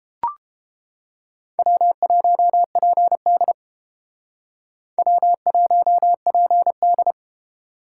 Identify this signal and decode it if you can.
Morse code, heard as a keyed tone, about 31 wpm, 700 Hz. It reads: W1PD W1PD